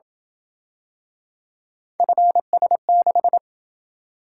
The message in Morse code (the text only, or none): FS6